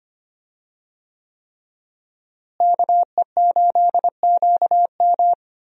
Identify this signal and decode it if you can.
Morse code, heard as a keyed tone, about 25 wpm, 700 Hz. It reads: KE8QM
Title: USA Call Signs - 5 Characters - Repeated 3x - 25wpm